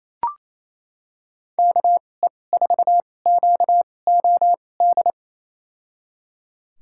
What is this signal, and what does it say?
Morse code, heard as a keyed tone, about 28 wpm, spaced as for 20 wpm, 700 Hz. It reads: KE4QOD
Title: USA Call Signs - 6 Characters - 2x Character Spacing - 28wpm